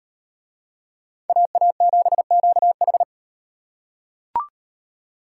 Morse code, heard as a keyed tone, about 37 wpm, 700 Hz. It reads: AA7QH